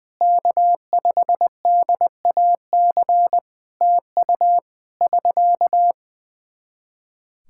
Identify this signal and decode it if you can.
Morse code, heard as a keyed tone, about 20 wpm, 700 Hz. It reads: K5DAC TU <SK>
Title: K1USN SST QSO Practice 20wpm